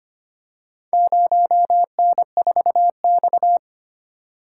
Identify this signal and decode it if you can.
Morse code, heard as a keyed tone, about 25 wpm, 700 Hz. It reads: 0N4X
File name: practice / 25wpm / Four Letter-Number Combinations - 25wpm